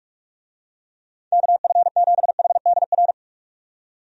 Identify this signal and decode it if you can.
Morse code, heard as a keyed tone, about 43 wpm, 700 Hz. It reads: KF7HDR
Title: USA Call Signs - 6 Characters - 45wpm